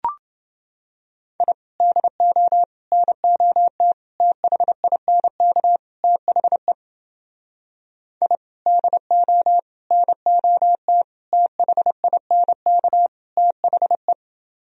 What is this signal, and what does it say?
Morse code, heard as a keyed tone, about 29 wpm, 700 Hz. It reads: I DO NOT THINK THE I DO NOT THINK THE